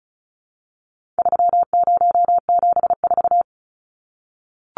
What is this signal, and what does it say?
Morse code, heard as a keyed tone, about 34 wpm, 700 Hz. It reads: 3074
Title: Numbers - 4-Digit 35wpm